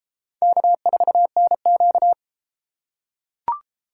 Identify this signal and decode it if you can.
Morse code, heard as a keyed tone, about 33 wpm, 700 Hz. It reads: K4NQ